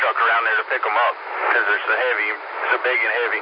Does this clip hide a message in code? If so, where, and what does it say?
Yes: Emergency Alert System (SAME) tones from 0.3 to 3.1 s, heard as harsh buzzing bursts, with speech.